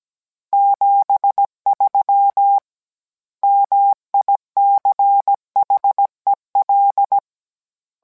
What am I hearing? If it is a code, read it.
Morse code, heard as a keyed tone, about 17 wpm, 800 Hz. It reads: 73 MICHEL